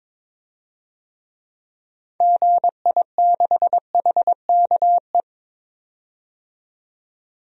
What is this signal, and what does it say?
Morse code, heard as a keyed tone, about 22 wpm, 700 Hz. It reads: GI6HKE
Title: DXCC-100 Call Signs - 6 Characters - 22wpm